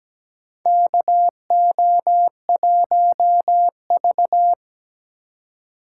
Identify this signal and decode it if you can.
Morse code, heard as a keyed tone, about 17 wpm, 700 Hz. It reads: KO1V
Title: USA Call Signs - 4 Characters - 4x Character Spacing - 17wpm